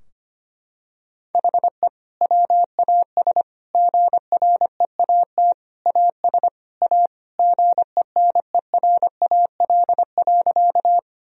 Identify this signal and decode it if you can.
Morse code, heard as a keyed tone, about 25 wpm, 700 Hz. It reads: HE WAS GREAT AS A GENERAL.
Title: Sentences from Top 1000 Words - Review Part 3 - 25wpm